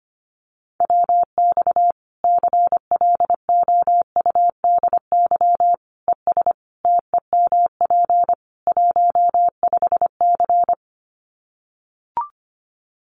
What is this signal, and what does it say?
Morse code, heard as a keyed tone, about 25 wpm, 700 Hz. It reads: WX CLOUDY ES TEMP 15C
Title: Common QSO Phrases 25wpm